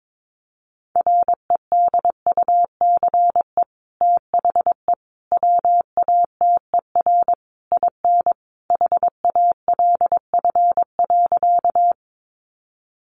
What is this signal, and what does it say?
Morse code, heard as a keyed tone, about 22 wpm, 700 Hz. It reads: REDUCE THE WATER IN HALF.
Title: Sentences from Top 1100 Words 22wpm